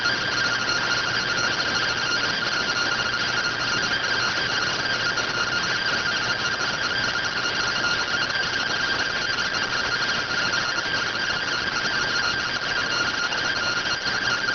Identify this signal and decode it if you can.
Two-tone teleprinter signal, about 170 Hz shift.